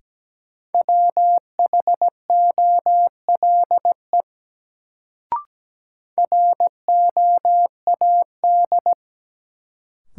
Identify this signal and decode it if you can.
Morse code, heard as a keyed tone, about 17 wpm, 700 Hz. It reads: WHOLE ROAD
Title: Sentences from Top 400 Words - New Word Drill - 17wpm